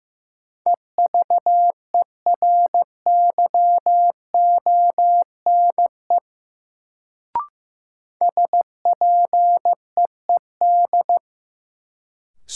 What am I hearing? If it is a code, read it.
Morse code, heard as a keyed tone, about 15 wpm, 700 Hz. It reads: EVERYONE SPEED
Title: Sentences from Top 800 Words - New Word Drill - 15wpm